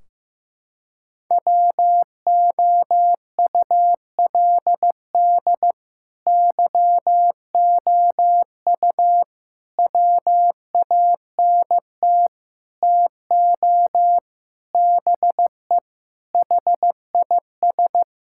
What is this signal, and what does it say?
Morse code, heard as a keyed tone, about 15 wpm, 700 Hz. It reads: WOULD YOU WANT TO BE HIS
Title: Sentences from Top 1600 Words 15wpm